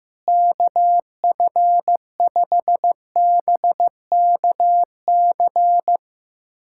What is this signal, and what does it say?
Morse code, heard as a keyed tone, about 15 wpm, 700 Hz. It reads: KF5BKC
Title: USA Callsign DE Callsign - 6 Characters - 15wpm